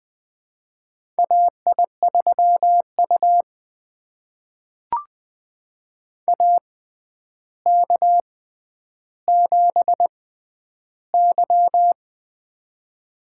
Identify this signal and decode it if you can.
Morse code, heard as a keyed tone, about 20 wpm, spaced as for 10 wpm, 700 Hz. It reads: AI3U AK7Y
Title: USA Call Signs - 4 Characters - 6x Character Spacing - 20wpm